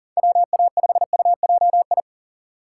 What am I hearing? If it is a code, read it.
Morse code, heard as a keyed tone, about 40 wpm, 700 Hz. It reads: WA5UJI
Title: USA Call Signs - 6 Characters - 6x Character Spacing - 40wpm